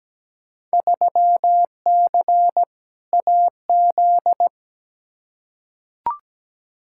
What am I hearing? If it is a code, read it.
Morse code, heard as a keyed tone, about 17 wpm, 700 Hz. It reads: 3C AZ
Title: Field Day Exchanges Encore - 17wpm